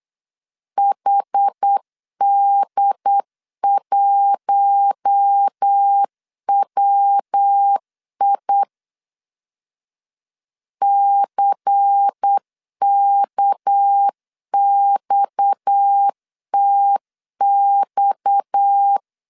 Morse code, heard as a keyed tone, about 8 wpm, 800 Hz. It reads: HD1WI CKXTX
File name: LEZIONI / 6 PKX0 / 6 PKX 8wpm 3041